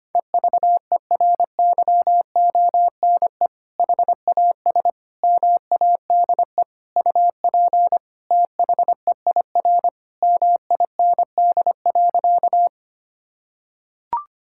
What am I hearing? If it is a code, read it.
Morse code, heard as a keyed tone, about 25 wpm, 700 Hz. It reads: EVERYONE HAS MADE UP THEIR MIND.
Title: Sentences from Top 1000 Words - Review Part 5 - 25wpm